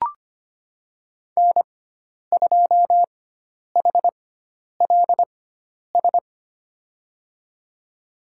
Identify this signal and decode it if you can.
Morse code, heard as a keyed tone, about 25 wpm, spaced as for 10 wpm, 700 Hz. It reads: N2HLS
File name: Lesson 16g - Letter c - Call Signs 25wpm